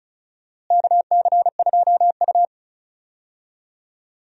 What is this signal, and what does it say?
Morse code, heard as a keyed tone, about 34 wpm, 700 Hz. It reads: KC2U
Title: USA Call Signs - 4 Characters - Repeated 2x - 35wpm